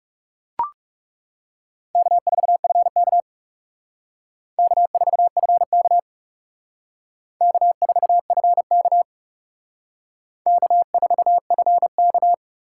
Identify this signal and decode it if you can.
Morse code, heard as a keyed tone, about 38 wpm, 700 Hz. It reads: K4FK K4FK K4FK K4FK
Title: Speed Racing - USA 1x2 Call Signs 30wpm